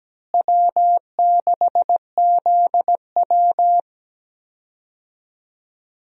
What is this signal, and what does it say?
Morse code, heard as a keyed tone, about 17 wpm, 700 Hz. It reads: W6ZW
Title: USA Call Signs - 4 Characters - Repeated 2x - 17wpm